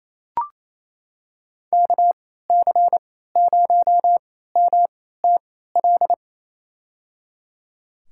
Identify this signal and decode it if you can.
Morse code, heard as a keyed tone, about 27 wpm, spaced as for 16 wpm, 700 Hz. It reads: KC0MTL
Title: USA Call Signs - 6 Characters - 3x Character Spacing - 28wpm